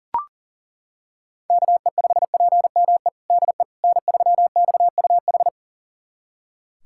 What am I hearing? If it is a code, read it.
Morse code, heard as a keyed tone, about 37 wpm, 700 Hz. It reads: KE5PME DE N3XUH
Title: USA Callsign DE Callsign 40wpm